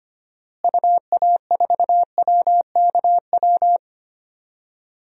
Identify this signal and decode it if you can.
Morse code, heard as a keyed tone, about 25 wpm, 700 Hz. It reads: UA4WKW
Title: DXCC-100 Call Signs - 6 Characters - Repeated 3x - 25wpm